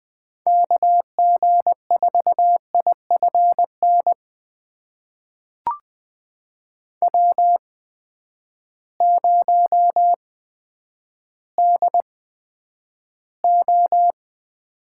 Morse code, heard as a keyed tone, about 20 wpm, spaced as for 10 wpm, 700 Hz. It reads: KG4IFN W0DO
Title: USA Call Signs - 8x Character Spacing - 20wpm